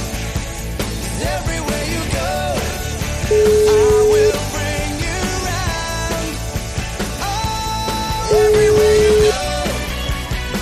Ringback tone (Europe, 425 Hz), with noise.